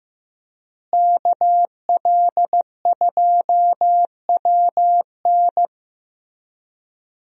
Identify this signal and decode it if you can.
Morse code, heard as a keyed tone, about 16 wpm, 700 Hz. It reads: KL2WN
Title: USA Call Signs - 5 Characters - Repeated 3x - 15wpm